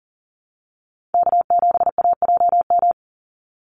Morse code, heard as a keyed tone, about 40 wpm, 700 Hz. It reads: K7AJM
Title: Call Signs - 40wpm